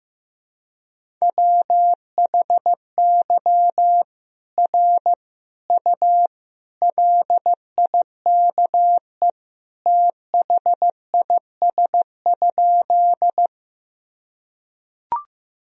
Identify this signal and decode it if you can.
Morse code, heard as a keyed tone, about 15 wpm, 700 Hz. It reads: WHY R U LIKE THIS?
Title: Sentences from Top 200 Words and Abbreviations 15wpm